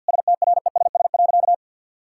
Morse code, heard as a keyed tone, about 50 wpm, 700 Hz. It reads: STRESS.
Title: Sentences from Top 1000 Words - Review Part 6 - 50wpm